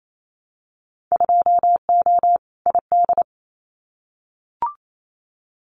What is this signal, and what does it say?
Morse code, heard as a keyed tone, about 28 wpm, 700 Hz. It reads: 2O ID
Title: Winter Field Day Exchanges 28wpm